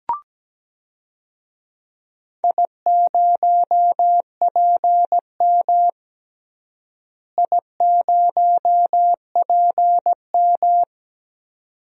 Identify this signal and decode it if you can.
Morse code, heard as a keyed tone, about 17 wpm, 700 Hz. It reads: I0PM I0PM
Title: DXCC-100 Call Signs - 4 Characters - Repeated 2x - 17wpm